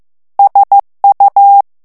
Morse code, heard as a keyed tone, about 15 wpm, 800 Hz. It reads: SU
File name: morse mp3 / ITALIEN texte clair 1 vitesse 14 mots minute